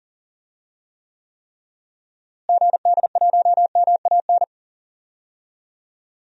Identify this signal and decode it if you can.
Morse code, heard as a keyed tone, about 40 wpm, 700 Hz. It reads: GD1MAN
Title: DXCC-100 Call Signs - 6 Characters - Repeated 3x - 40wpm